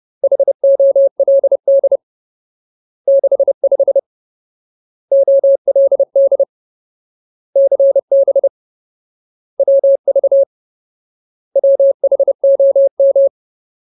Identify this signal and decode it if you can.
Morse code, heard as a keyed tone, about 30 wpm, 550 Hz. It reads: HOLD 65 OLD CB WV WHOM
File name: Ultimate Word List / 30WPM Ultimate